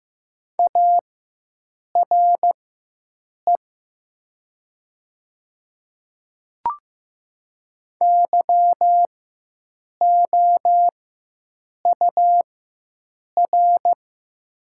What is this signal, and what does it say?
Morse code, heard as a keyed tone, about 15 wpm, spaced as for 7 wpm, 700 Hz. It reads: ARE YOUR
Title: 4x Character Spacing - Top 100 Words - 15wpm